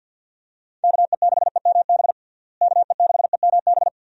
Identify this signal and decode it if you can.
Morse code, heard as a keyed tone, about 51 wpm, 700 Hz. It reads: KE6EMB KE6EMB
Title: USA Call Signs - 6 Characters - Repeated 2x - 50wpm